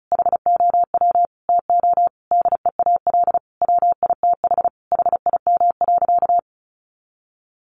Morse code, heard as a keyed tone, about 34 wpm, 700 Hz. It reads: HOW TO DEAL WITH HIM.
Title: Sentences from Top 800 Words 35wpm